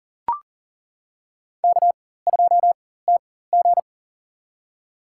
Morse code, heard as a keyed tone, about 40 wpm, spaced as for 19 wpm, 700 Hz. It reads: K2TG
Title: USA Call Signs - 4x Character Spacing - 40wpm